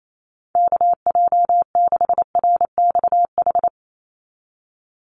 Morse code, heard as a keyed tone, about 28 wpm, 700 Hz. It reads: KJ6RXH
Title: USA Call Signs 28wpm